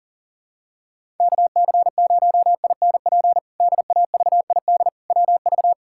Morse code, heard as a keyed tone, about 37 wpm, 700 Hz. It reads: KC0INP DAVID WV
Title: K1USN SST QSO Practice - 40wpm